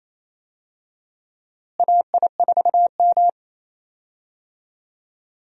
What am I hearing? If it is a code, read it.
Morse code, heard as a keyed tone, about 28 wpm, 700 Hz. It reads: AI4M
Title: USA Call Signs - 4 Characters - Repeated 3x - 28wpm